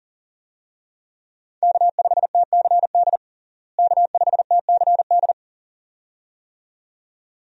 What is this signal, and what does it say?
Morse code, heard as a keyed tone, about 40 wpm, 700 Hz. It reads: K5TCD K5TCD
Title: USA Call Signs - 5 Characters - Repeated 2x - 40wpm